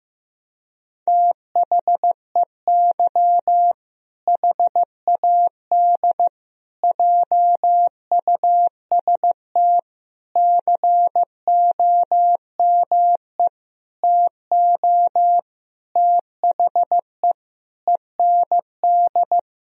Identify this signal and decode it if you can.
Morse code, heard as a keyed tone, about 15 wpm, 700 Hz. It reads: THEY HAD JUST COME TO THE END
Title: Sentences from Top 1000 Words - Review Part 2 - 15wpm